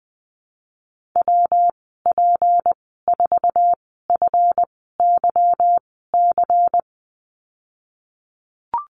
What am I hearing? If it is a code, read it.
Morse code, heard as a keyed tone, about 20 wpm, spaced as for 14 wpm, 700 Hz. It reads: WP4FYC
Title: 161 Lesson 26g - Letter g - Call Signs 20wpm